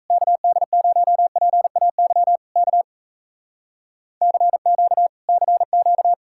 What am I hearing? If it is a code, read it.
Morse code, heard as a keyed tone, about 39 wpm, 700 Hz. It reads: KD0PAY K CQ CQ